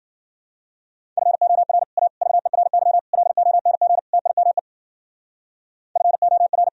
Morse code, heard as a keyed tone, about 50 wpm, 700 Hz. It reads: FOR A VERY LONG TIME FOR
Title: Speed-Racing - Sets of 5 Words 40wpm